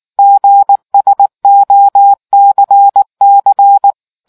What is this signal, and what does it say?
Morse code, heard as a keyed tone, about 19 wpm, 800 Hz. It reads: GSOCC